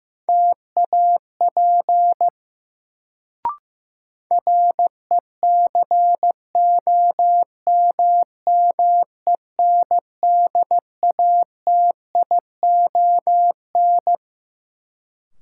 Morse code, heard as a keyed tone, about 15 wpm, 700 Hz. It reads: TAP RECOMMENDATION
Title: Sentences from Top 1400 Words - New Word Drill Alt - 15wpm